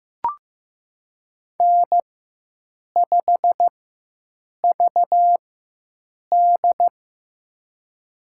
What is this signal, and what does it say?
Morse code, heard as a keyed tone, about 15 wpm, spaced as for 7 wpm, 700 Hz. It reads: N5VD
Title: USA Call Signs - 4 Characters - 4x Character Spacing - 15wpm